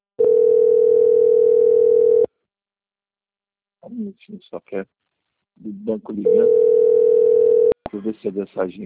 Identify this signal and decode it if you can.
Ringback tone (North America), 440 + 480 Hz, with speech.
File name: out- - - - - .849